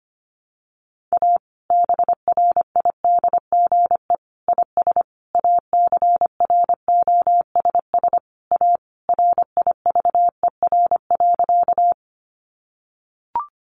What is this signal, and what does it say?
Morse code, heard as a keyed tone, about 25 wpm, 700 Hz. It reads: A BRIDGE IS ACROSS A RIVER.
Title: Sentences from Top 1200 Words 25wpm